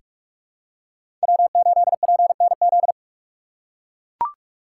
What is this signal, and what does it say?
Morse code, heard as a keyed tone, about 43 wpm, 700 Hz. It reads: W8PNZ
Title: USA Call Signs - 2x Character Spacing - 45wpm